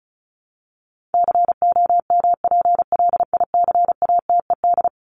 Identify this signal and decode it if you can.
Morse code, heard as a keyed tone, about 34 wpm, 700 Hz. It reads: COMPLICATED
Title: Top 1001-1250 Words 35wpm